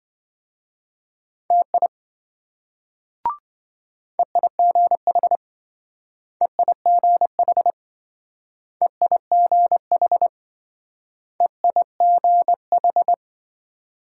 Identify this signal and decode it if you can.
Morse code, heard as a keyed tone, about 26 wpm, 700 Hz. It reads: TI EIGH EIGH EIGH EIGH